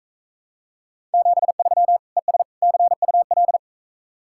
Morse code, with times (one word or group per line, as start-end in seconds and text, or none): 1.1-2.0 73
2.2-2.4 ES
2.6-3.6 CUL